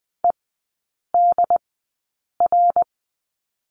Morse code, heard as a keyed tone, about 20 wpm, spaced as for 8 wpm, 700 Hz. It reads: EDR
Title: 074 Lesson 13e - Letter L - Three Characters 20wpm